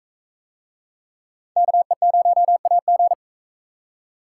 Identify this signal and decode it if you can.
Morse code, heard as a keyed tone, about 38 wpm, 700 Hz. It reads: KE0AG